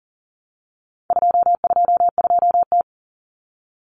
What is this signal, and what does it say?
Morse code, heard as a keyed tone, about 40 wpm, 700 Hz. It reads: 222T
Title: Common Cut Numbers - 4-Digit 40wpm